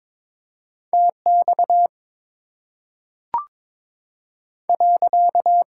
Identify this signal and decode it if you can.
Morse code, heard as a keyed tone, about 22 wpm, 700 Hz. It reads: T X .
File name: Top 90 QSO Elements - 8x Character Spacing - 22wpm